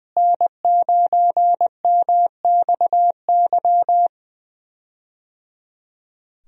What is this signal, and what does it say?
Morse code, heard as a keyed tone, about 20 wpm, 700 Hz. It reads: N9MXY
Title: USA Call Signs - 5 Characters - Repeated 3x - 20wpm